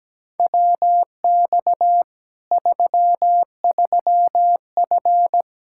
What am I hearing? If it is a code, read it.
Morse code, heard as a keyed tone, about 17 wpm, 700 Hz. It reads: WX 33F
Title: Practice QSO - Exchange 2 - Info - 17wpm